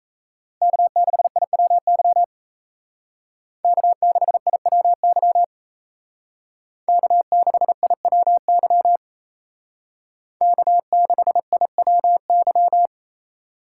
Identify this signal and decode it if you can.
Morse code, heard as a keyed tone, about 36 wpm, 700 Hz. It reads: K6IWY K6IWY K6IWY K6IWY